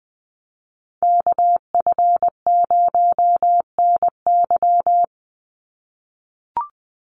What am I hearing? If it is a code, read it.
Morse code, heard as a keyed tone, about 20 wpm, 700 Hz. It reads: KF0NY
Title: USA Call Signs 20wpm